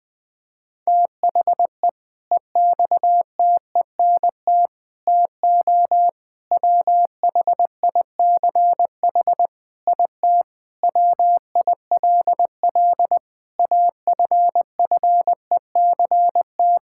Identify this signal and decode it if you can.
Morse code, heard as a keyed tone, about 20 wpm, 700 Hz. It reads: THE EXTENT TO WHICH IT WILL AFFECT
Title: Sentences with 4-Word Phrases - 20wpm